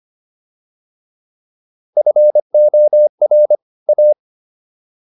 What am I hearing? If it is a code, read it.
Morse code, heard as a keyed tone, about 25 wpm, 600 Hz. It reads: FOR A